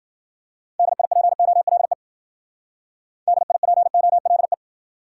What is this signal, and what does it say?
Morse code, heard as a keyed tone, about 50 wpm, 700 Hz. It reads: DIPOLE DIPOLE
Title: Speed-Racing Top 90 QSO Elements 40wpm